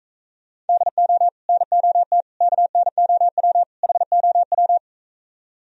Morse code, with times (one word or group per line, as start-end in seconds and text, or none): 0.7-1.3 DO
1.5-2.2 NOT
2.4-3.6 KNOW
3.8-4.8 HOW